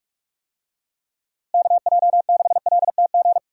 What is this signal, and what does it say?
Morse code, heard as a keyed tone, about 46 wpm, 700 Hz. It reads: KJ6LTG
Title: USA Call Signs - 6 Characters - Repeated 2x - 45wpm